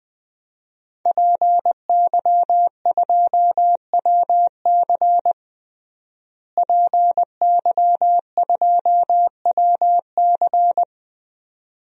Morse code, heard as a keyed tone, about 21 wpm, 700 Hz. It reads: PY2WC PY2WC
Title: DXCC-100 Call Signs - 5 Characters - Repeated 2x - 20wpm